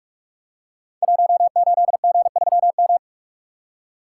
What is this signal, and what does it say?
Morse code, heard as a keyed tone, about 46 wpm, 700 Hz. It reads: 18G3M
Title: Code Groups - 5 Letter:Numbers - 45wpm